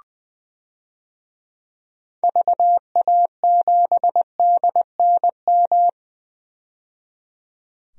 Morse code, heard as a keyed tone, about 20 wpm, 700 Hz. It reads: VA7DNM
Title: DXCC-100 Call Signs - 6 Characters - 20wpm